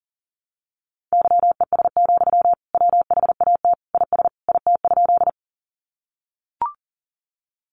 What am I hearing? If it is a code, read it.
Morse code, heard as a keyed tone, about 40 wpm, 700 Hz. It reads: YES, WHAT IS IT?
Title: Sentences from Top 500 Words 40wpm